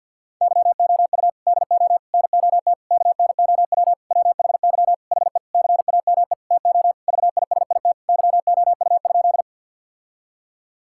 Morse code, heard as a keyed tone, about 50 wpm, 700 Hz. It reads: YOU DO NOT KNOW WHY HE CAME TO VISIT YOU?
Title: Sentences from Top 1000 Words - Review Part 6 - 50wpm